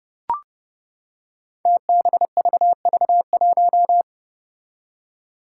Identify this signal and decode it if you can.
Morse code, heard as a keyed tone, about 30 wpm, 700 Hz. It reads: TBVV1
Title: Five Challenging Letter-Number Combinations 30wpm